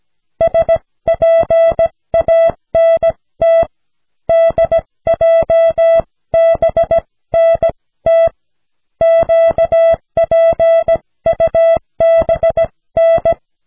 Morse code, heard as a keyed tone, about 16 wpm, spaced as for 14 wpm, 660 Hz. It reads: SPANT DJBNT QPUBN
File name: cw / snelheid 4